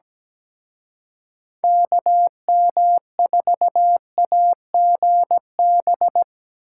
Morse code, heard as a keyed tone, about 21 wpm, 700 Hz. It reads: KM4AGB
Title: Speed Racing - USA 2x3 Call Signs 17wpm